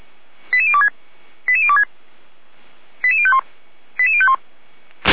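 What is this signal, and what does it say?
Five-tone selective call (ZVEI), heard as quick runs of five tones, tones of about 70 ms. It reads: 88026 88026 88051 88051
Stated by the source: ZVEI five-tone call 88074 received by SDR, from the Feuerwehr Orlamünde audio server.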